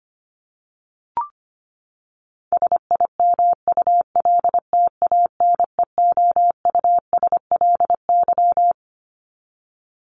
Morse code, heard as a keyed tone, about 25 wpm, 700 Hz. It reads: SIMULTANEOUSLY
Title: LY Suffix - Top 100 Words with Base Words 25wpm